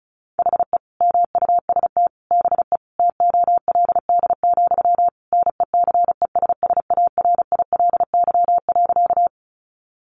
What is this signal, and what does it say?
Morse code, heard as a keyed tone, about 34 wpm, 700 Hz. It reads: HE MUST BE TOLD, NECESSARILY.
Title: Sentences from Top 1400 Words 35wpm